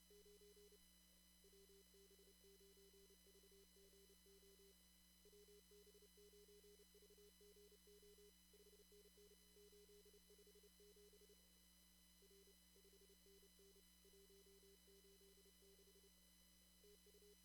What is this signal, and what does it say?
Morse code, heard as a keyed tone, about 29 wpm, 410 Hz, with noise. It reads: 9 WB9VGO WB9VGO 5NN 857 R 5NN 186 TU
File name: CW / kt5j wpx cw-op-k5pi